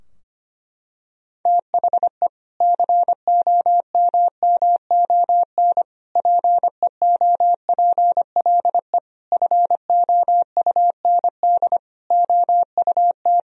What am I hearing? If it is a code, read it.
Morse code, heard as a keyed tone, about 25 wpm, 700 Hz. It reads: THE COMMON PEOPLE FOUND OUT
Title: Sentences from Top 1000 Words - Review Part 3 - 25wpm